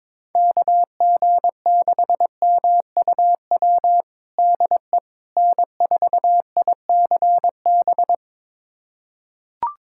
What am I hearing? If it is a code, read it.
Morse code, heard as a keyed tone, about 22 wpm, 700 Hz. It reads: KG6MUW DE N4ICB
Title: USA Callsign DE Callsign 22wpm